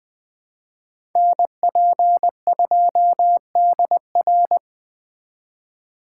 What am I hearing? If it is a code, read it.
Morse code, heard as a keyed tone, about 20 wpm, 700 Hz. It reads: NP2DR
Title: USA Call Signs - 5 Characters - 2x Character Spacing - 20wpm